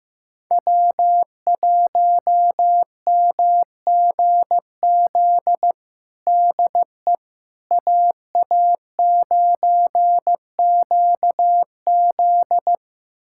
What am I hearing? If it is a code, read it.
Morse code, heard as a keyed tone, about 15 wpm, 700 Hz. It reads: W1MGZ DE AA9QZ
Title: USA Callsign DE Callsign - 5 Characters - 15wpm